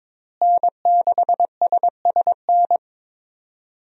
Morse code, heard as a keyed tone, about 22 wpm, 700 Hz. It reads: N6SSN
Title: USA Call Signs - 5 Characters - Repeated 3x - 22wpm